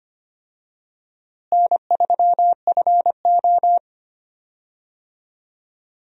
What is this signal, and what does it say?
Morse code, heard as a keyed tone, about 25 wpm, 700 Hz. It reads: N3FO
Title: USA Call Signs - 4 Characters - Repeated 2x - 25wpm